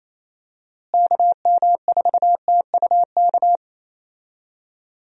Morse code, heard as a keyed tone, about 28 wpm, 700 Hz. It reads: KM4TUK
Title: USA Call Signs - 6 Characters - Repeated 3x - 28wpm